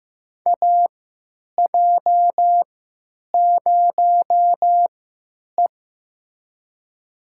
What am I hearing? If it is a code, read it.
Morse code, heard as a keyed tone, about 15 wpm, spaced as for 9 wpm, 700 Hz. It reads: AJ0E
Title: USA Call Signs - 3x Character Spacing - 15wpm